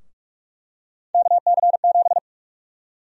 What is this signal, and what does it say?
Morse code, heard as a keyed tone, about 43 wpm, 700 Hz. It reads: KC7
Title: USA Call Sign Prefixes - 45wpm